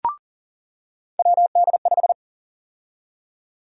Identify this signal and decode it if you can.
Morse code, heard as a keyed tone, about 40 wpm, 700 Hz. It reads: WD5
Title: USA Call Sign Prefixes - 40wpm